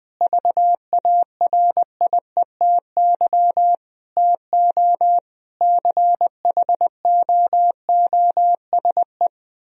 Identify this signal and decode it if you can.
Morse code, heard as a keyed tone, about 20 wpm, 700 Hz. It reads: VARIETY TO CHOOSE
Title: Sentences from Top 1000 Words - Review Part 4 - 20wpm